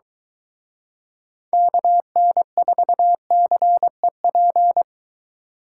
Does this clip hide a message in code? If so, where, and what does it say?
Yes: Morse code from 1.5 to 4.8 s, heard as a keyed tone, about 23 wpm, 700 Hz. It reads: KN4CEP